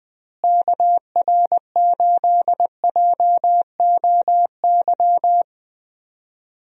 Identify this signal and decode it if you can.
Morse code, heard as a keyed tone, about 20 wpm, 700 Hz. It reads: KR8JOY
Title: USA Call Signs - 6 Characters - Repeated 3x - 20wpm